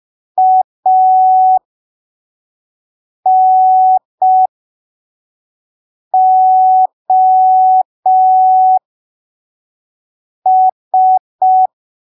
Morse code, heard as a keyed tone, about 5 wpm, spaced as for 3 wpm, 750 Hz. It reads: ANOS
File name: arquivos / texto / texto 08 05ppm